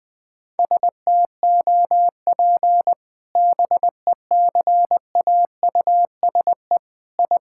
Morse code, heard as a keyed tone, about 21 wpm, 700 Hz. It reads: STOP BECAUSE I
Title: Sentences with 2-Word Phrases - 20wpm